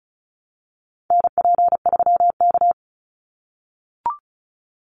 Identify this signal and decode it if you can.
Morse code, heard as a keyed tone, about 34 wpm, 700 Hz. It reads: NP3K
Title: Call Signs - 35wpm